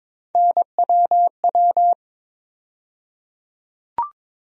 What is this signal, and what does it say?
Morse code, heard as a keyed tone, about 21 wpm, 700 Hz. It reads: NWW
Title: USA Call Sign Postfixes - 3 Letters 22wpm